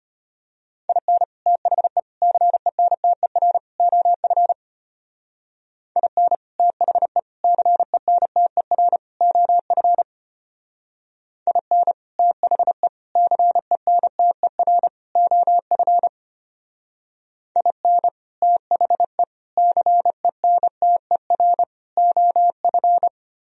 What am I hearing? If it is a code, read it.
Morse code, heard as a keyed tone, about 33 wpm, 700 Hz. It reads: IN THE CENTER OF IN THE CENTER OF IN THE CENTER OF IN THE CENTER OF